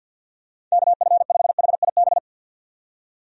Morse code, heard as a keyed tone, about 50 wpm, 700 Hz. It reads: KF5HIB
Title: USA Call Signs - 6 Characters - 50wpm